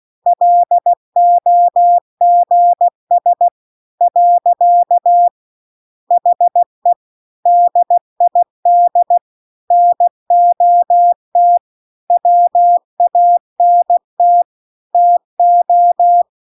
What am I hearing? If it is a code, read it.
Morse code, heard as a keyed tone, about 16 wpm, 700 Hz. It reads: LOGS . HE DID NOT WANT TO
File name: CW-The-Big-Twoheaded-River-2-16WPM